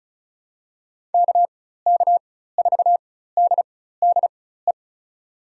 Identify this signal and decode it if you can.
Morse code, heard as a keyed tone, about 34 wpm, spaced as for 16 wpm, 700 Hz. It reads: KK4DDE
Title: USA Call Signs - 6 Characters - 4x Character Spacing - 35wpm